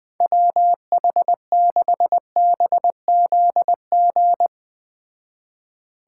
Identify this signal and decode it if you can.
Morse code, heard as a keyed tone, about 20 wpm, 700 Hz. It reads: WH6BZG